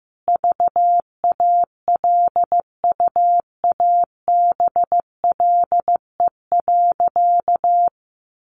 Morse code, heard as a keyed tone, about 15 wpm, 700 Hz. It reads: VALUABLE.
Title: Sentences from Top 1100 Words 15wpm